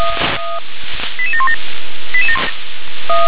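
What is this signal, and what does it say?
Five-tone selective call (ZVEI), heard as quick runs of five tones, tones of about 80 ms. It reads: 88717
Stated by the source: SDR-received ZVEI five-tone selective call, address 88711.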